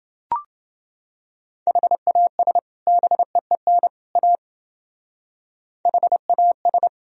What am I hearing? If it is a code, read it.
Morse code, heard as a keyed tone, about 29 wpm, 700 Hz. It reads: HAS BEEN A HAS